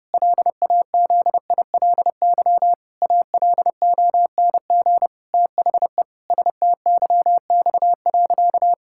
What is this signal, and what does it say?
Morse code, heard as a keyed tone, about 30 wpm, 700 Hz. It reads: LAZILY ALONG THE STYX.
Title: Pangram Challenge v2 - 30wpm